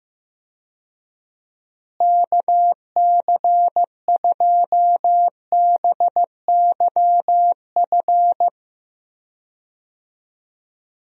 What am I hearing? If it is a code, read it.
Morse code, heard as a keyed tone, about 15 wpm, 700 Hz. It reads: KC2BYF
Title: USA Call Signs - 6 Characters - Repeated 3x - 15wpm